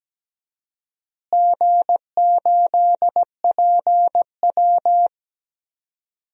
Morse code, heard as a keyed tone, about 17 wpm, 700 Hz. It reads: G8PW
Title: DXCC-100 Call Signs - 4 Characters - Repeated 2x - 17wpm